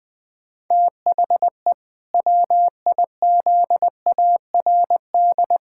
Morse code, heard as a keyed tone, about 21 wpm, 700 Hz. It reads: THE WIZARD
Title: Pangram Challenge v2 - 20wpm